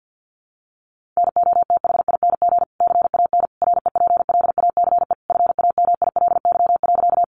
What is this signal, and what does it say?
Morse code, heard as a keyed tone, about 50 wpm, 700 Hz. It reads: NOTHING CAN REPLACE FAMILY.
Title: Sentences from Top 1400 Words 50wpm